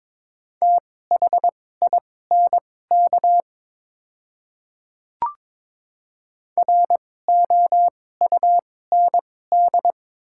Morse code, heard as a keyed tone, about 22 wpm, spaced as for 16 wpm, 700 Hz. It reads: THINK ROUND
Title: 2x Character Spacing - Top 200 Words - 22wpm